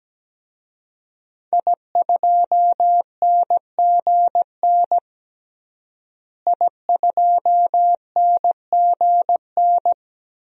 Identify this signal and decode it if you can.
Morse code, heard as a keyed tone, about 17 wpm, 700 Hz. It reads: I2NGN I2NGN
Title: DXCC-100 Call Signs - 5 Characters - Repeated 2x - 17wpm